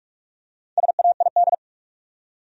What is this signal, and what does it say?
Morse code, heard as a keyed tone, about 45 wpm, 700 Hz. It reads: SAID
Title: Top 100 Words 45wpm